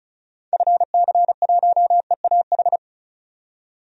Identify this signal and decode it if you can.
Morse code, heard as a keyed tone, about 34 wpm, 700 Hz. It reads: FC1EAH
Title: DXCC-100 Call Signs - 6 Characters - 35wpm